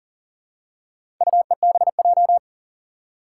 Morse code, heard as a keyed tone, about 39 wpm, 700 Hz. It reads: UEBJ
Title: Four Challenging Letter-Number Combinations 40wpm